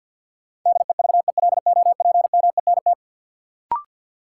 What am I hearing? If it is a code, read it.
Morse code, heard as a keyed tone, about 53 wpm, 700 Hz. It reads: DEVELOPMENT
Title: Sets of 3 Words - New Word Drill - 50wpm